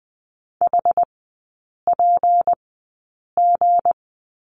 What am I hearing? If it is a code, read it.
Morse code, heard as a keyed tone, about 20 wpm, spaced as for 8 wpm, 690 Hz. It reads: HPG
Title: 166 Lesson 27e - Number 7 - Three Characters 20wpm